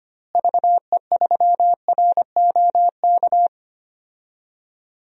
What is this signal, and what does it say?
Morse code, heard as a keyed tone, about 25 wpm, 700 Hz. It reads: VE3ROK
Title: DXCC-100 Call Signs - 6 Characters - Repeated 2x - 25wpm